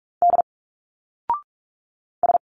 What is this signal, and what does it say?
Morse code, heard as a keyed tone, about 46 wpm, 700 Hz. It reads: D S
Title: Single Letters 45wpm